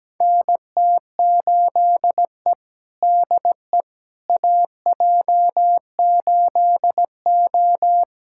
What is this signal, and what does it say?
Morse code, heard as a keyed tone, about 17 wpm, 700 Hz. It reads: NT8E DE AJ8O
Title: USA Callsign DE Callsign - 4 Characters - 17wpm